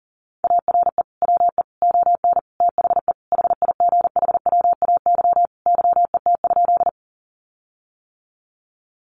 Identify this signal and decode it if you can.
Morse code, heard as a keyed tone, about 40 wpm, 700 Hz. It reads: ARE WE ON THE HIGHWAY YET?
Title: Sentences from Top 1400 Words 40wpm